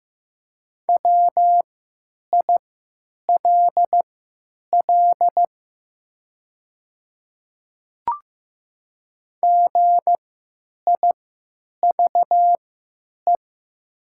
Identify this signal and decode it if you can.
Morse code, heard as a keyed tone, about 15 wpm, spaced as for 8 wpm, 700 Hz. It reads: WILL GIVE
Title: 3x Character Spacing - Top 200 Words - 15wpm